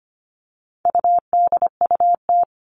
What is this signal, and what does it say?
Morse code, heard as a keyed tone, about 25 wpm, 700 Hz. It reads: UDUT
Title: All Cut Numbers - 4-Digit 25wpm